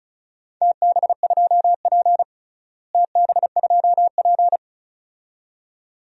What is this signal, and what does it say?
Morse code, heard as a keyed tone, about 36 wpm, 700 Hz. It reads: TB2P TB2P
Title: DXCC-100 Call Signs - 4 Characters - Repeated 2x - 35wpm